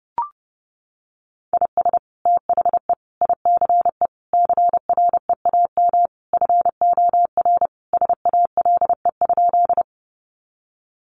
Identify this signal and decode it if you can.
Morse code, heard as a keyed tone, about 30 wpm, 700 Hz. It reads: IS THE ICE CREAM FOR SALE?
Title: Sentences from Top 1500 Words 30wpm